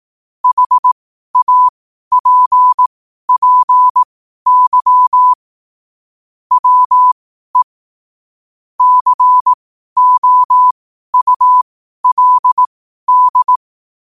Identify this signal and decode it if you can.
Morse code, heard as a keyed tone, about 18 wpm, spaced as for 12 wpm, 1000 Hz. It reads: HAPPY WE COULD
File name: qotdfiles / 12wpm